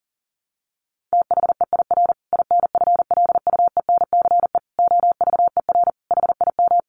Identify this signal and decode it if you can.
Morse code, heard as a keyed tone, about 40 wpm, 700 Hz. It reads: THEIR INFLUENCE OVER HIM
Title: Sentences from Top 1400 Words 40wpm